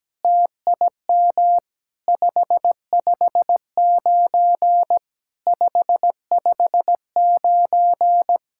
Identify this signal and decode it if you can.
Morse code, heard as a keyed tone, about 17 wpm, 700 Hz. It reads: TIM 559 559
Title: SOTA QSO Practice 17wpm